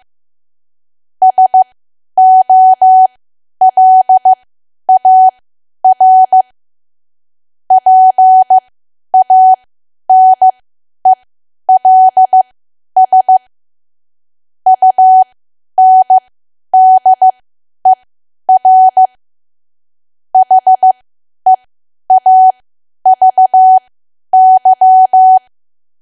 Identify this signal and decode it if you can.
Morse code, heard as a keyed tone, about 15 wpm, spaced as for 10 wpm, 750 Hz. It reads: SOLAR PANELS UNDER HEAVY
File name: Archive / 10 WPM / 10WPM